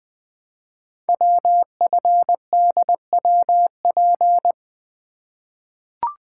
Morse code, heard as a keyed tone, about 20 wpm, 700 Hz. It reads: WFDWP
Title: Code Groups - 5 Letters - 20wpm